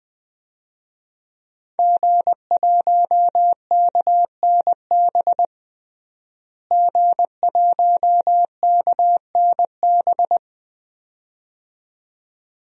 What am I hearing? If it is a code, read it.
Morse code, heard as a keyed tone, about 20 wpm, 700 Hz. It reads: G1KNB G1KNB
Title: DXCC-100 Call Signs - 5 Characters - Repeated 2x - 20wpm